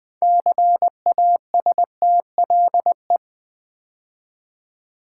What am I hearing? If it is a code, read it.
Morse code, heard as a keyed tone, about 20 wpm, 700 Hz. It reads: CASTLE
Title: 6 Letter Words - First 3 Letters Given - 20wpm